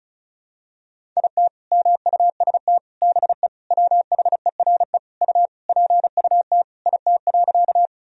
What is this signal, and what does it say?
Morse code, heard as a keyed tone, about 36 wpm, 700 Hz. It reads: IT MUST BE WHERE U PUT IT.
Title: Sentences from Top 200 Words and Abbreviations 35wpm